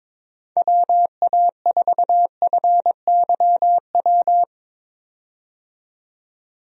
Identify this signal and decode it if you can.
Morse code, heard as a keyed tone, about 22 wpm, 700 Hz. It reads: WA4FYW
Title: USA Call Signs - 6 Characters - 22wpm